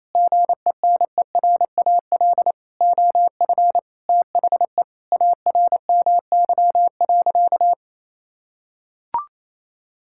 Morse code, heard as a keyed tone, about 27 wpm, 700 Hz. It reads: GENERAL OF THE ARMY.
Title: Sentences from Top 1600 Words 28wpm